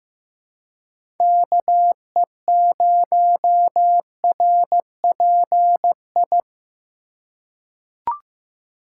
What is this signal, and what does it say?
Morse code, heard as a keyed tone, about 15 wpm, 700 Hz. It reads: KE0RPI
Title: USA Call Signs - 6 Characters - 2x Character Spacing - 15wpm